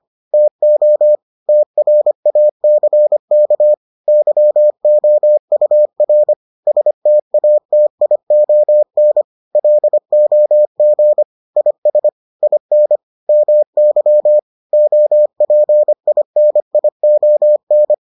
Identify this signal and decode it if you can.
Morse code, heard as a keyed tone, about 25 wpm, 600 Hz. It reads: TO TRACK YOUR STATION LOG IS IN MY OPINION